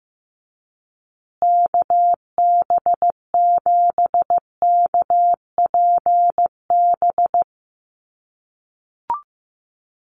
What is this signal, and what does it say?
Morse code, heard as a keyed tone, about 15 wpm, 700 Hz. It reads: KB7KPB
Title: USA Call Signs Repeated 3x 15wpm